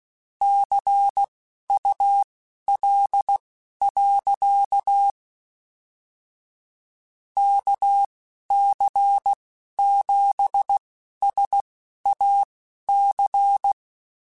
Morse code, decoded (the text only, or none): CUL. KC7SAC